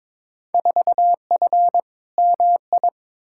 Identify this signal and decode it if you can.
Morse code, heard as a keyed tone, about 21 wpm, 700 Hz. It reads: 4F MI
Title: Field Day Exchanges Encore - 22wpm